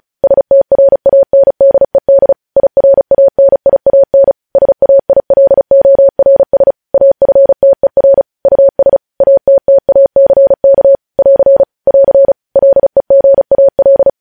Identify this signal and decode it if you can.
Morse code, heard as a keyed tone, about 36 wpm, 550 Hz. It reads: STRANDED IRANIAN SAILORS AFTER US ATTACK <AR> <AR> LEGAL